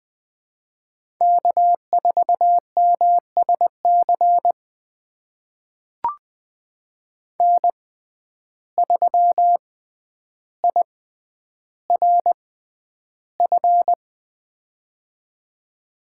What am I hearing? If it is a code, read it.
Morse code, heard as a keyed tone, about 20 wpm, spaced as for 10 wpm, 700 Hz. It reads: K4MSC N3IRF
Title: USA Call Signs - 5 Characters - 6x Character Spacing - 20wpm